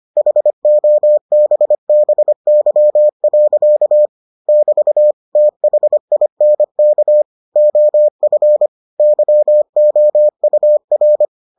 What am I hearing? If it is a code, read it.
Morse code, heard as a keyed tone, about 25 wpm, 600 Hz. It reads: HOBBY. <BT> THINK OF YOUR